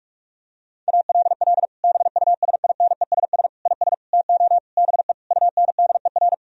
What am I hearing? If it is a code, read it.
Morse code, heard as a keyed tone, about 46 wpm, 700 Hz. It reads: ALL BUSINESS IS TO BE UNDER
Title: Sentences from Top 1000 Words - Review Part 3 - 45wpm